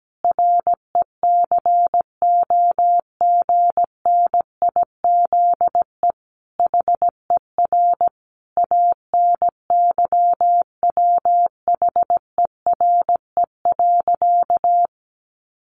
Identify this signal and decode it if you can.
Morse code, heard as a keyed tone, about 17 wpm, 700 Hz. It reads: RECOGNIZE HER ANYWHERE.
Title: Sentences from Top 1400 Words 17wpm